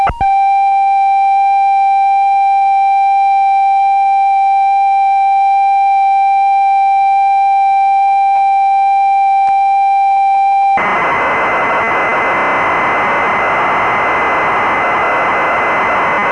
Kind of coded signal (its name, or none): computer data on cassette